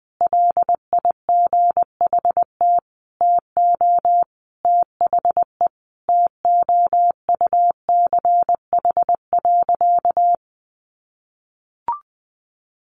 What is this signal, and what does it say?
Morse code, heard as a keyed tone, about 20 wpm, 700 Hz. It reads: LIGHT TO THE TOUCH.
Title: Sentences from Top 1500 Words 20wpm